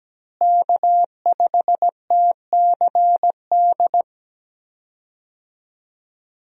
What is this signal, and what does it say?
Morse code, heard as a keyed tone, about 17 wpm, 700 Hz. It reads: K5TCD
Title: USA Call Signs - 5 Characters - Repeated 2x - 17wpm